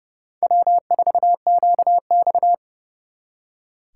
Morse code, heard as a keyed tone, about 34 wpm, 700 Hz. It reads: W4QX
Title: Speed Racing - USA 1x2 Call Signs 30wpm